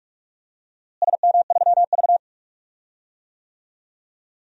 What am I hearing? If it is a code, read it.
Morse code, heard as a keyed tone, about 46 wpm, 700 Hz. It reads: SM3V
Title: DXCC-100 Call Signs - 4 Characters - Repeated 3x - 45wpm